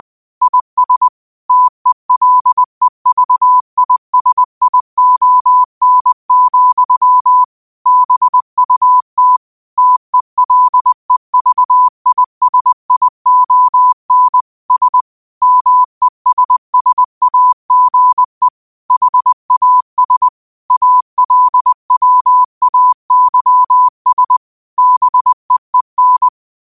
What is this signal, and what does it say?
Morse code, heard as a keyed tone, about 20 wpm, 1000 Hz. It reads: IS TELEVISION, BUT TELEVISION S MESSAGE HAS ALWAYS BEEN